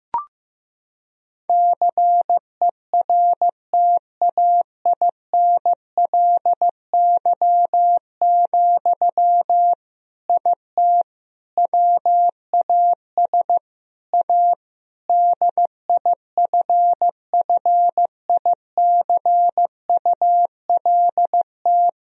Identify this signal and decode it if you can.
Morse code, heard as a keyed tone, about 15 wpm, 700 Hz. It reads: CERTAINLY, IT WAS A DIFFICULT
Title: Sentences from Top 1000 Words - Review Part 2 - 15wpm